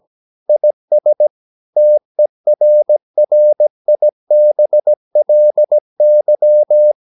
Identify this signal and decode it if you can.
Morse code, heard as a keyed tone, about 17 wpm, 600 Hz. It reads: IS TERRIBLY
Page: Romanen Never af Ken Follet på engelsk.